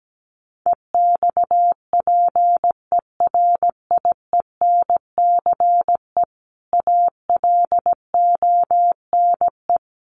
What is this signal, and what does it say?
Morse code, heard as a keyed tone, about 17 wpm, 700 Hz. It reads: EXPERIENCE ALONE
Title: Sentences from Top 1000 Words 17wpm